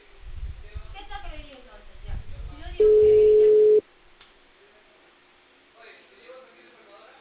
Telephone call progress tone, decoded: ringback tone

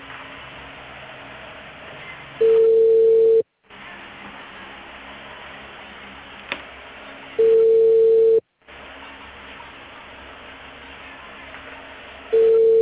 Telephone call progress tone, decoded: ringback tone

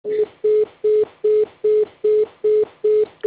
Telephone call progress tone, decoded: congestion tone

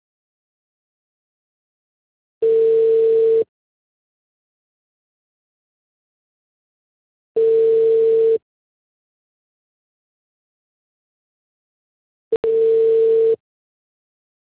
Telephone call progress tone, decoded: ringback tone

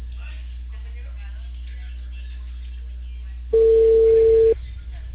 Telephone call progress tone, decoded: ringback tone

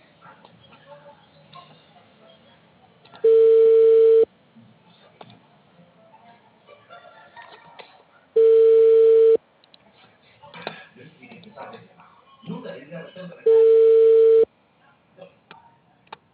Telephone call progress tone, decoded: ringback tone

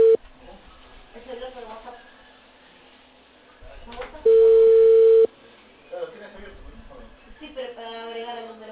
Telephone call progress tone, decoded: ringback tone

